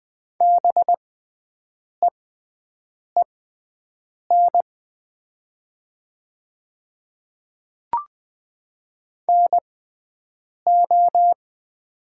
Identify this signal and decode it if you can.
Morse code, heard as a keyed tone, about 20 wpm, spaced as for 7 wpm, 700 Hz. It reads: BEEN NO
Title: 6x Character Spacing - Top 100 Words - 20wpm